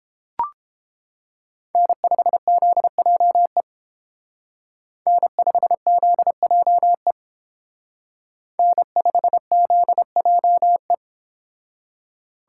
Morse code, heard as a keyed tone, about 28 wpm, 700 Hz. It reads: N5ZJE N5ZJE N5ZJE